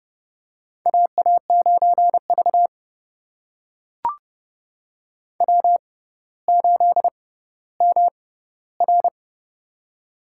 Morse code, heard as a keyed tone, about 30 wpm, spaced as for 15 wpm, 700 Hz. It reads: AA9V W8MR
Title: USA Call Signs - 4 Characters - 6x Character Spacing - 30wpm